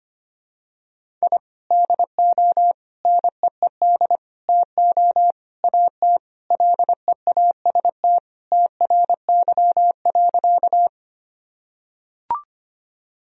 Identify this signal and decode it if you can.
Morse code, heard as a keyed tone, about 25 wpm, 700 Hz. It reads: I DO NEED TO AT LEAST TRY.
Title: Sentences from Top 1000 Words - Review Part 1 - 25wpm